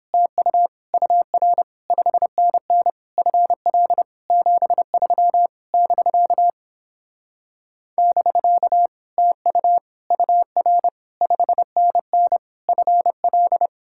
Morse code, heard as a keyed tone, about 29 wpm, 700 Hz. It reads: TU UR 5NN FL 73 <BK> <BK> TU UR 5NN FL